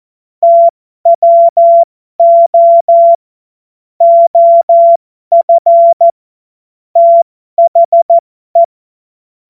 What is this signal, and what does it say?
Morse code, heard as a keyed tone, about 14 wpm, spaced as for 12 wpm, 680 Hz. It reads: TWO OF THE
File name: Morsekurs / Witze